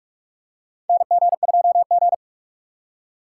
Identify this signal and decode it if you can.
Morse code, heard as a keyed tone, about 46 wpm, 700 Hz. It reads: NG2G
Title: USA Call Signs - 4 Characters - 4x Character Spacing - 45wpm